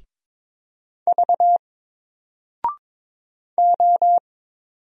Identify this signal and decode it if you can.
Morse code, heard as a keyed tone, about 22 wpm, 700 Hz. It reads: V O
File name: Single-Letter-Number-22wpm